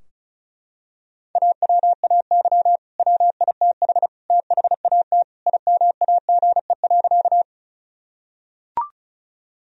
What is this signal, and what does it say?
Morse code, heard as a keyed tone, about 36 wpm, 700 Hz. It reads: AWAY WITH THAT IMAGE.
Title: Sentences from Top 1000 Words - Review Part 4 - 35wpm